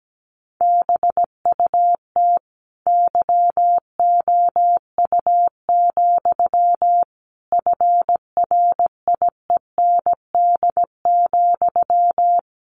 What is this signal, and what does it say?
Morse code, heard as a keyed tone, about 17 wpm, 700 Hz. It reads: BUT YOU, FRIEND,
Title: A Sound Of Thunder 17wpm